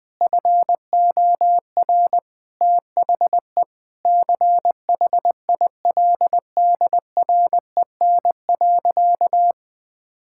Morse code, heard as a keyed tone, about 20 wpm, 700 Hz. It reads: FOR THE CHILDREN.
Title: Sentences from Top 1000 Words - Review Part 6 - 20wpm